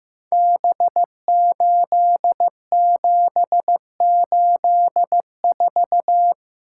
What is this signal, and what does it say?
Morse code, heard as a keyed tone, about 15 wpm, 700 Hz. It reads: B8784
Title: Code Groups - 5 Letter:Numbers - 15wpm